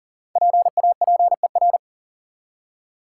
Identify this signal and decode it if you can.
Morse code, heard as a keyed tone, about 37 wpm, 700 Hz. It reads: PAPER
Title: Five Letter Words - 40wpm